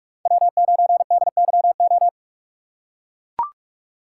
Morse code, heard as a keyed tone, about 42 wpm, 700 Hz. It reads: W9DYO